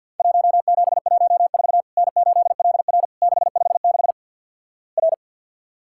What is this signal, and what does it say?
Morse code, heard as a keyed tone, about 50 wpm, 700 Hz. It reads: 1714 N8LR 656 R
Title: Worked All Europe DX Contest - QTC Exchange - 50wpm